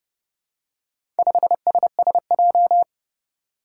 Morse code, heard as a keyed tone, about 30 wpm, 700 Hz. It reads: 5SSJ
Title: Four Challenging Letter-Number Combinations 30wpm